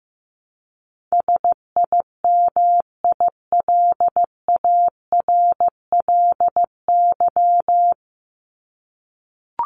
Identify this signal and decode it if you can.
Morse code, heard as a keyed tone, about 15 wpm, 700 Hz. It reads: SIMILARLY
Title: Top 1251-1500 Words 15wpm